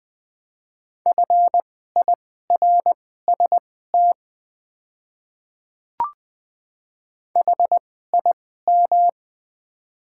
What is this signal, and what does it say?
Morse code, heard as a keyed tone, about 20 wpm, spaced as for 14 wpm, 700 Hz. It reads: FIRST HIM
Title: 2x Character Spacing - Top 100 Words - 20wpm